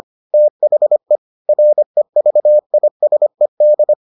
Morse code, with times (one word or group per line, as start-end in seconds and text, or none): 0.3-1.2 THE
1.5-3.9 REVISED